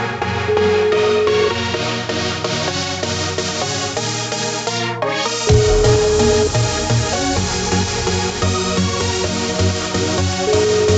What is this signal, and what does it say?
Ringback tone (Europe, 425 Hz), with noise.